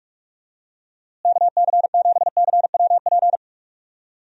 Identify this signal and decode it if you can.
Morse code, heard as a keyed tone, about 41 wpm, 700 Hz. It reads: KC7CWP
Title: USA Call Signs - 6 Characters - Repeated 3x - 45wpm